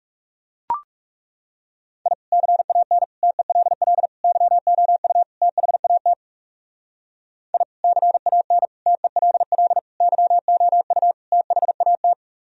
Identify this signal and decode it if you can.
Morse code, heard as a keyed tone, about 36 wpm, 700 Hz. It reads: I CAN TELL YOU THAT I CAN TELL YOU THAT
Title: Speed-Racing - Sets of 5 Words 30wpm